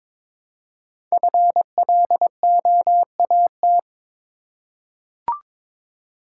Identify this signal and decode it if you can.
Morse code, heard as a keyed tone, about 22 wpm, 700 Hz. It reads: FLOAT
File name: 5 Letter Words - First 3 Letters Given - 22wpm